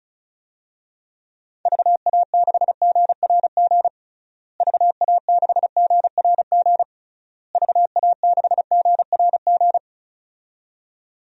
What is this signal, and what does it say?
Morse code, heard as a keyed tone, about 34 wpm, 700 Hz. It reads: VA6GRG VA6GRG VA6GRG
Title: DXCC-100 Call Signs - 6 Characters - Repeated 3x - 35wpm